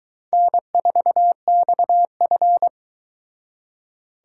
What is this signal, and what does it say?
Morse code, heard as a keyed tone, about 22 wpm, 700 Hz. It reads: N4XF